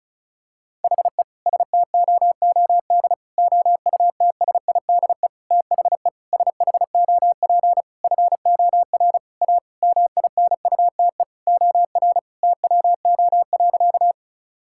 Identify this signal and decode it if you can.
Morse code, heard as a keyed tone, about 36 wpm, 700 Hz. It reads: HE STOOD OUTSIDE THE SHOP FOR A MINUTE OR TWO.
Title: Sentences from Top 1600 Words 35wpm